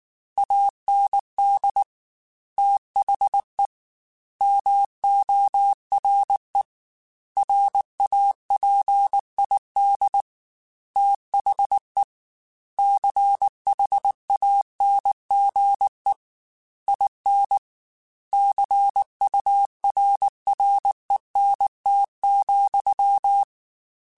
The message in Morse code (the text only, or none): AND THE MORE RAPID THE CHANGE IN CURRENT,